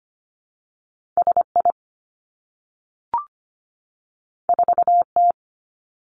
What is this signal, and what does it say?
Morse code, heard as a keyed tone, about 25 wpm, 700 Hz. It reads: S I 4 T
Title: Two Challenging Letter-Number Combinations 25wpm